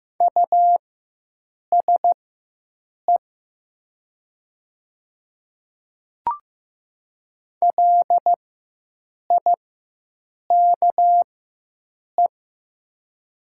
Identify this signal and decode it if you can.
Morse code, heard as a keyed tone, about 15 wpm, spaced as for 7 wpm, 700 Hz. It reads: USE LIKE
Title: 4x Character Spacing - Top 100 Words - 15wpm